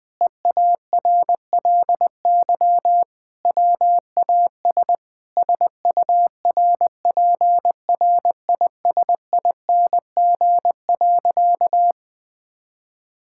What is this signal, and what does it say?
Morse code, heard as a keyed tone, about 21 wpm, 700 Hz. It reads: EARLY WAS SURPRISING.
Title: Sentences with 4-Word Phrases - 20wpm